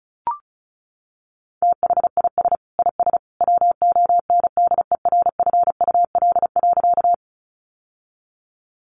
Morse code, heard as a keyed tone, about 34 wpm, 700 Hz. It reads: THIS IS WONDERFUL.
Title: Sentences from Top 1100 Words 35wpm